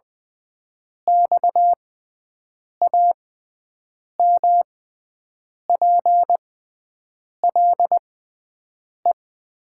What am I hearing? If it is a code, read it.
Morse code, heard as a keyed tone, about 20 wpm, spaced as for 7 wpm, 700 Hz. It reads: XAMPLE
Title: 6x Character Spacing - Top 300 Words - 20wpm